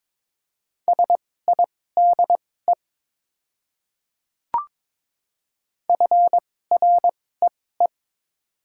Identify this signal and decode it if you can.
Morse code, heard as a keyed tone, about 22 wpm, spaced as for 16 wpm, 700 Hz. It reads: SIDE FREE
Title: 2x Character Spacing - Top 500 Words - 22wpm